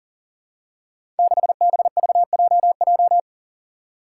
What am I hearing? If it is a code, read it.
Morse code, heard as a keyed tone, about 38 wpm, 700 Hz. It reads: 6BVJJ
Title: Five Challenging Letter-Number Combinations 40wpm